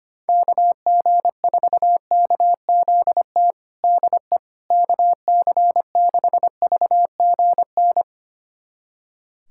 Morse code, heard as a keyed tone, about 25 wpm, 700 Hz. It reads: KG4KZT DE KC6VGN
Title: USA Callsign DE Callsign 25wpm